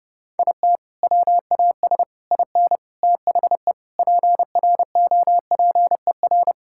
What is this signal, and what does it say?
Morse code, heard as a keyed tone, about 30 wpm, 700 Hz. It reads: IT WAS IN THE PROPER
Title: Sentences from Top 1000 Words - Review Part 5 - 30wpm